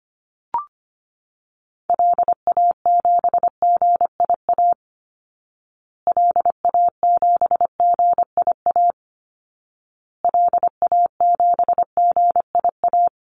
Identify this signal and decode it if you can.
Morse code, heard as a keyed tone, about 25 wpm, 700 Hz. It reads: LA7GIA LA7GIA LA7GIA
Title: Call Signs Repeated 2x - 25wpm